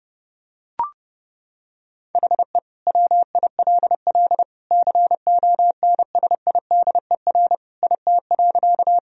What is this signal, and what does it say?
Morse code, heard as a keyed tone, about 32 wpm, 700 Hz. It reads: HE WILL CONSIDER IT.
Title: Sentences from Top 1000 Words - Review Part 4 - 30wpm